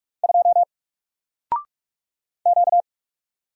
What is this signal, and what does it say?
Morse code, heard as a keyed tone, about 44 wpm, 700 Hz. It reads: 2 Q
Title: Single Letter-Number Alt 45wpm